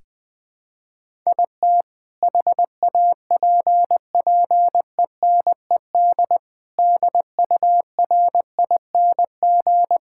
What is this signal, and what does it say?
Morse code, heard as a keyed tone, about 20 wpm, 700 Hz. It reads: IT HAPPENED DURING
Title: Sentences with 2-Word Phrases - 20wpm